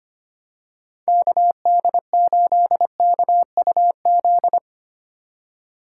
Morse code, heard as a keyed tone, about 25 wpm, 700 Hz. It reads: KD8KUZ
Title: USA Call Signs - 6 Characters - 2x Character Spacing - 25wpm